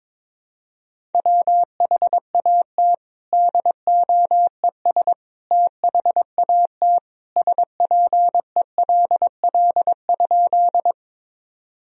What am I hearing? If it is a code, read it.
Morse code, heard as a keyed tone, about 22 wpm, 700 Hz. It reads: WHAT DOES THAT SPELL?
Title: Sentences from Top 1000 Words - Review Part 1 - 22wpm